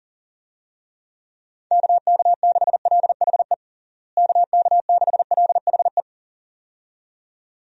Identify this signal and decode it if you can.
Morse code, heard as a keyed tone, about 40 wpm, 700 Hz. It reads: KK6LHE KK6LHE
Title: USA Call Signs - 6 Characters - Repeated 2x - 40wpm